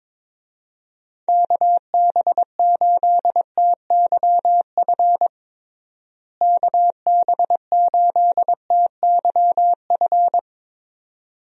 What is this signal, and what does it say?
Morse code, heard as a keyed tone, about 22 wpm, 700 Hz. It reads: KB8TYF KB8TYF
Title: USA Call Signs - 6 Characters - Repeated 2x - 22wpm